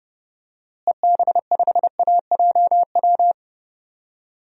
Morse code, heard as a keyed tone, about 30 wpm, 700 Hz. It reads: EB5AJW
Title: DXCC-100 Call Signs - 6 Characters - Repeated 3x - 30wpm